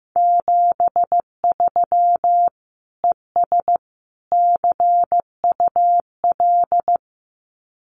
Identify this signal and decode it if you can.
Morse code, heard as a keyed tone, about 15 wpm, 700 Hz. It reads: 73 ES CUL
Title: Common QSO Phrases Encore 15wpm